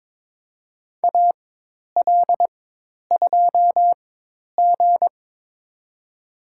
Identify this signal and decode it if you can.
Morse code, heard as a keyed tone, about 22 wpm, spaced as for 10 wpm, 700 Hz. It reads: AL2G
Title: USA Call Signs - 4 Characters - 4x Character Spacing - 22wpm